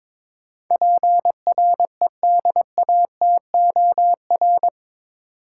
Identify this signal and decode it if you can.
Morse code, heard as a keyed tone, about 22 wpm, 700 Hz. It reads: PREDATOR
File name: practice / 22wpm / Eight Letter Words - 22wpm